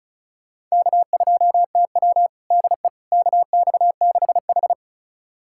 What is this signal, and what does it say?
Morse code, heard as a keyed tone, about 34 wpm, 700 Hz. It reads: K2TW DE KX6H
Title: USA Callsign DE Callsign - 4 Characters - 35wpm